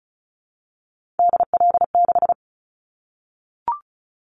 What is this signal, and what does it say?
Morse code, heard as a keyed tone, about 36 wpm, 700 Hz. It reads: DL6
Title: Callsign Prefixes for Top 50 DXCC - 35wpm